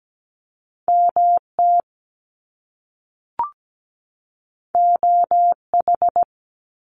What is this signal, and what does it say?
Morse code, heard as a keyed tone, about 17 wpm, 700 Hz. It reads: MT OH
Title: US States 17wpm